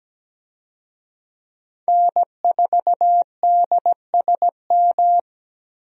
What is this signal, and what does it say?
Morse code, heard as a keyed tone, about 17 wpm, 700 Hz. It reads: N4DSM
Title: USA Call Signs - 5 Characters - Repeated 3x - 17wpm